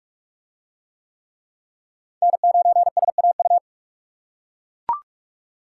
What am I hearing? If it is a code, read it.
Morse code, heard as a keyed tone, about 44 wpm, 700 Hz. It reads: N9SAU